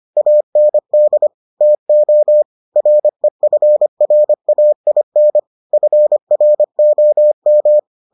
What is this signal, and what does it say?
Morse code, heard as a keyed tone, about 25 wpm, 600 Hz. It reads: AND TO REFRAIN FROM